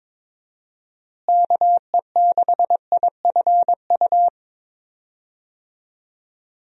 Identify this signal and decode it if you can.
Morse code, heard as a keyed tone, about 22 wpm, 700 Hz. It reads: KE6IFU
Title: USA Call Signs - 6 Characters - Repeated 2x - 22wpm